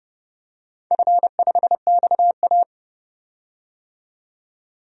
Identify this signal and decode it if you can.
Morse code, heard as a keyed tone, about 30 wpm, 700 Hz. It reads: F5XA
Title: DXCC-100 Call Signs - 4 Characters - 30wpm